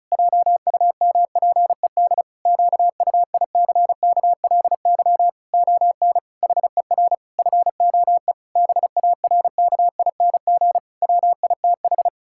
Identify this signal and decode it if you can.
Morse code, heard as a keyed tone, about 34 wpm, 700 Hz. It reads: JUMPED QUICKLY ON HER FOE BARKING WITH
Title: Pangram Challenge v2 - 35wpm